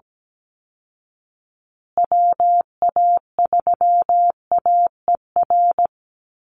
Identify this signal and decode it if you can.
Morse code, heard as a keyed tone, about 17 wpm, 700 Hz. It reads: WA3AER
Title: Call Signs Repeated 2x - 17wpm